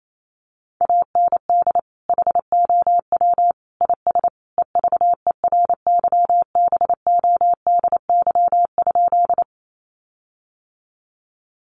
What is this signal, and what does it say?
Morse code, heard as a keyed tone, about 28 wpm, 700 Hz. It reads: AND HOW IS EVERYBODY?
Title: Sentences from Top 1400 Words 28wpm